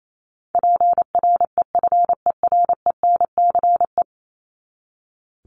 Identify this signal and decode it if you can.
Morse code, heard as a keyed tone, about 28 wpm, 700 Hz. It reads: PREFERENCE
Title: Top 1251-1500 Words 28wpm